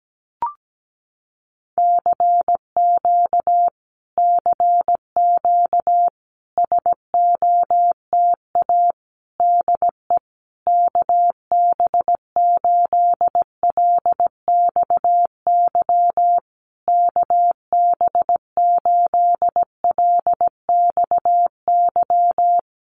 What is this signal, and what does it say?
Morse code, heard as a keyed tone, about 17 wpm, 700 Hz. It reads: CQ CQ SOTA DE KB8LXY KB8LXY
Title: Common QSO Phrases Encore 17wpm